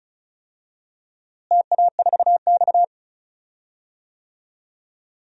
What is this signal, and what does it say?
Morse code, heard as a keyed tone, about 36 wpm, 700 Hz. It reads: TA4X
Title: DXCC-100 Call Signs - 4 Characters - Repeated 3x - 35wpm